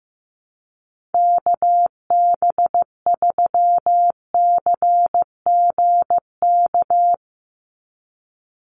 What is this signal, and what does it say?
Morse code, heard as a keyed tone, about 15 wpm, 700 Hz. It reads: KB3CGK
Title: USA Call Signs 15wpm